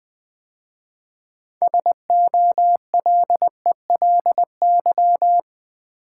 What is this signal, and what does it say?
Morse code, heard as a keyed tone, about 20 wpm, 700 Hz. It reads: SOLELY
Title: 6 Letter Words - First 3 Letters Given - 20wpm